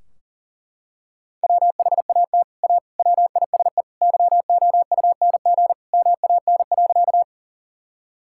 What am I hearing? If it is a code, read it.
Morse code, heard as a keyed tone, about 40 wpm, 700 Hz. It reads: WHAT A WISE YOUNG MAN.
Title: Sentences from Top 1600 Words 40wpm